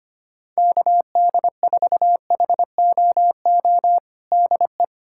Morse code, heard as a keyed tone, about 25 wpm, 700 Hz. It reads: KD4HOO DE
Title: Practice QSO - Exchange 4 - Ending - 25wpm